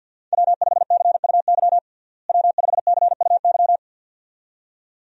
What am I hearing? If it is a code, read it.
Morse code, heard as a keyed tone, about 52 wpm, 700 Hz. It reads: W5CUY W5CUY
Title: USA Call Signs - 5 Characters - Repeated 2x - 50wpm